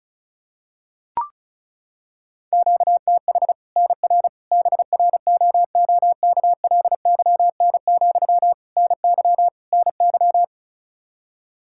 Morse code, heard as a keyed tone, about 36 wpm, 700 Hz. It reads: QTH NR BROOKLYN, NY NY
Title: Practice QSO - Exchange 1 - Beginning - 35wpm